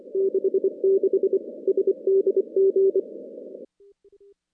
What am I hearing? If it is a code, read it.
Morse code, heard as a keyed tone, about 26 wpm, 410 Hz. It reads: 66 SDG
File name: contests / CW-SS-k5tr- -2100